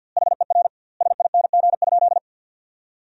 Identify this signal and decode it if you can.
Morse code, heard as a keyed tone, about 50 wpm, 700 Hz. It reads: HER SING?
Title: 2x Word-Spacing - Sentences from Top 400 Words - 50wpm